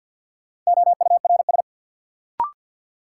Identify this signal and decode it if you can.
Morse code, heard as a keyed tone, about 53 wpm, 700 Hz. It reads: OURS
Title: Sentences from Top 1300 Words - New Word Drill Alt - 50wpm